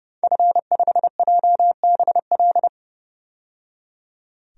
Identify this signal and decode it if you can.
Morse code, heard as a keyed tone, about 30 wpm, 700 Hz. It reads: F5JBL
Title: DXCC-100 Call Signs - 5 Characters - 30wpm